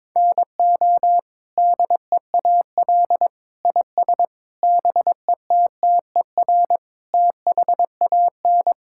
Morse code, heard as a keyed tone, about 22 wpm, 700 Hz. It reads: NO DEAL IS BETTER THAN
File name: practice / 22wpm / Sets of 8 Words - 22wpm